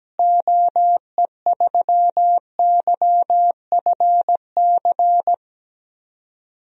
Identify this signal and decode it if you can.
Morse code, heard as a keyed tone, about 17 wpm, 700 Hz. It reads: OE3YFC
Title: DXCC-100 Call Signs - 6 Characters - Repeated 2x - 17wpm